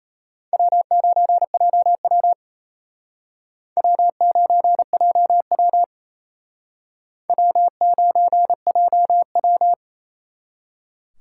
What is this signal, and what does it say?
Morse code, heard as a keyed tone, about 33 wpm, 700 Hz. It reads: W9JW W9JW W9JW
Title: Speed Racing - USA 1x2 Call Signs 28wpm